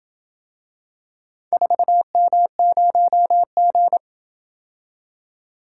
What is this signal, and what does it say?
Morse code, heard as a keyed tone, about 26 wpm, 700 Hz. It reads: 4M0G